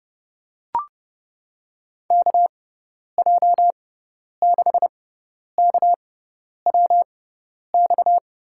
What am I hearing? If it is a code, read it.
Morse code, heard as a keyed tone, about 30 wpm, spaced as for 10 wpm, 700 Hz. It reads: KJ6KWX
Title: USA Call Signs - 6 Characters - 6x Character Spacing - 30wpm